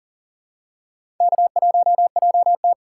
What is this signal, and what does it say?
Morse code, heard as a keyed tone, about 40 wpm, 700 Hz. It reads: K1JT
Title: USA Call Signs - 4x Character Spacing - 40wpm